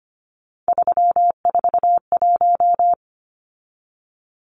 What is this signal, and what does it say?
Morse code, heard as a keyed tone, about 25 wpm, 700 Hz. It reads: 341
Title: Numbers - 3-Digit 25wpm